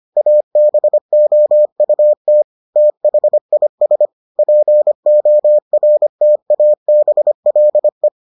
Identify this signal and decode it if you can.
Morse code, heard as a keyed tone, about 25 wpm, 600 Hz. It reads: ABOUT THIS PORTABLE